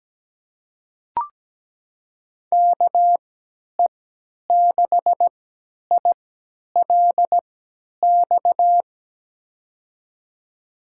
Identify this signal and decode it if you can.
Morse code, heard as a keyed tone, about 17 wpm, spaced as for 10 wpm, 700 Hz. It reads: KE6ILX
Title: USA Call Signs - 6 Characters - 3x Character Spacing - 17wpm